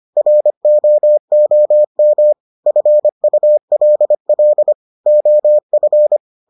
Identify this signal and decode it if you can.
Morse code, heard as a keyed tone, about 25 wpm, 600 Hz. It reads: ROOM FULL OF